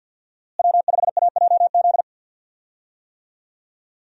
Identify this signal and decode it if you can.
Morse code, heard as a keyed tone, about 50 wpm, 700 Hz. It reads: W5AJZ
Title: USA Call Signs - 5 Characters - Repeated 3x - 50wpm